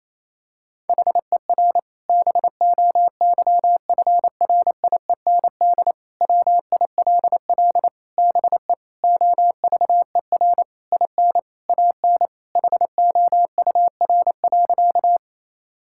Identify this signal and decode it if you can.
Morse code, heard as a keyed tone, about 28 wpm, 700 Hz. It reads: HER BOYFRIEND WILL BE OVER IN AN HOUR.
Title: Sentences from Top 1000 Words - Review Part 5 - 28wpm